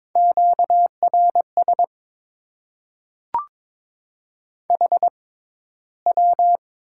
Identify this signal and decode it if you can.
Morse code, heard as a keyed tone, about 21 wpm, spaced as for 13 wpm, 700 Hz. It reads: QRS HW
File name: Top 90 QSO Elements - 6x Character Spacing - 22wpm